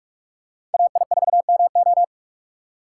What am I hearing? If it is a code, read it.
Morse code, heard as a keyed tone, about 46 wpm, 700 Hz. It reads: AI4MO
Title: USA Call Signs - 3x Character Spacing - 45wpm